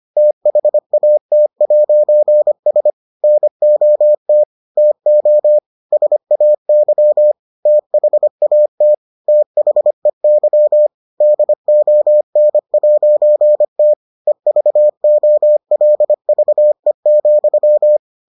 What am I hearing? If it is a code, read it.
Morse code, heard as a keyed tone, about 25 wpm, 600 Hz. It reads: THAT'S NOT TO SAY THAT THEY DON'T EVOLVE,